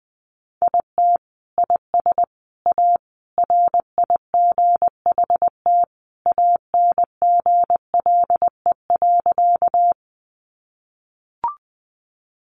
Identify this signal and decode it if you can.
Morse code, heard as a keyed tone, about 20 wpm, 700 Hz. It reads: IT IS A RIGHT ANGLE.
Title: Sentences from Top 1500 Words 20wpm